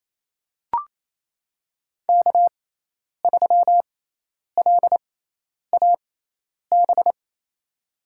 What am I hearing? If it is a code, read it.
Morse code, heard as a keyed tone, about 27 wpm, spaced as for 10 wpm, 700 Hz. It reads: K3LAB
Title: USA Call Signs - 6x Character Spacing - 28wpm